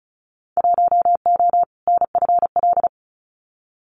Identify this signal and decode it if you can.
Morse code, heard as a keyed tone, about 36 wpm, 690 Hz. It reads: 1O NFL
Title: Winter Field Day Exchanges 35wpm